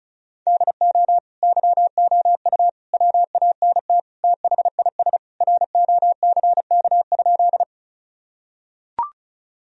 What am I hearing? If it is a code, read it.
Morse code, heard as a keyed tone, about 34 wpm, 700 Hz. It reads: DO YOU WANT THIS ROCK?
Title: Sentences from Top 1000 Words - Review Part 2 - 35wpm